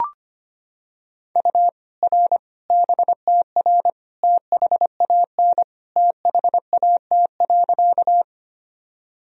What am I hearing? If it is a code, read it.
Morse code, heard as a keyed tone, about 25 wpm, 700 Hz. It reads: U R BTR THAN THAT.
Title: Sentences from Top 400 Words and Abbreviations 25wpm